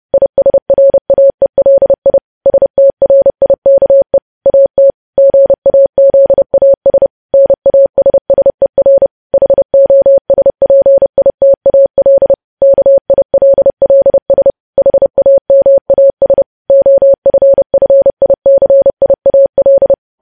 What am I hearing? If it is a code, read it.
Morse code, heard as a keyed tone, about 30 wpm, 550 Hz. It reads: ISRAELI STRIKE AT GAZAS NASSER HOSPITAL KILLS HAMAS OFFICIAL
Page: news headlines in morse code at 30 wpm